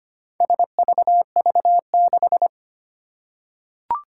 Morse code, heard as a keyed tone, about 25 wpm, 700 Hz. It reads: SVV6
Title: Four Challenging Letter-Number Combinations 25wpm